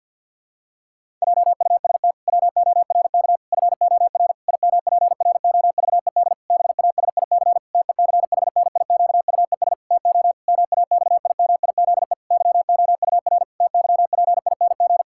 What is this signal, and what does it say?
Morse code, heard as a keyed tone, about 51 wpm, 700 Hz. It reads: JUST WORK FOR IMPROVED BASIC TECHNIQUES TO MAXIMIZE YOUR TYPING
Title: Pangram Challenge v2 - 50wpm